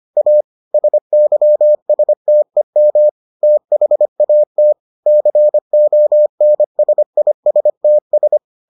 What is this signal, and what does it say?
Morse code, heard as a keyed tone, about 25 wpm, 600 Hz. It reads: A SYSTEM THAT CONSISTS